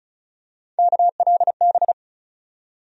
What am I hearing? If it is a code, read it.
Morse code, heard as a keyed tone, about 34 wpm, 700 Hz. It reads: KLB
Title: USA Call Sign Postfixes - 3 Letters 35wpm